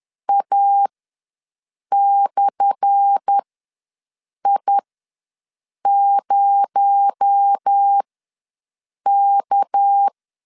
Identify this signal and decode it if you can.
Morse code, heard as a keyed tone, about 10 wpm, spaced as for 6 wpm, 800 Hz. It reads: A/I0K
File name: LEZIONI / 10 SIGNS / 10A SLASH ws 10wpm 4121